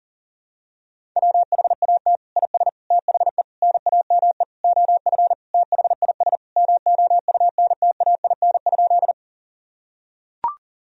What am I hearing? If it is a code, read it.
Morse code, heard as a keyed tone, about 40 wpm, 700 Hz. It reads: WHAT IS THE NAME OF THIS MOUNTAIN?
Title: Sentences from Top 1000 Words - Review Part 1 - 40wpm